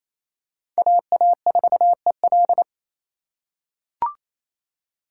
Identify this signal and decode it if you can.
Morse code, heard as a keyed tone, about 28 wpm, 700 Hz. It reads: AA4EL
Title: USA Call Signs - 2x Character Spacing - 28wpm